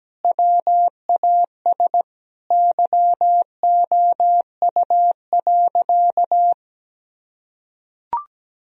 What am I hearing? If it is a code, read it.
Morse code, heard as a keyed tone, about 17 wpm, 700 Hz. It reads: WAS YOU.
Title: Sentences from Top 1000 Words - Review Part 3 - 17wpm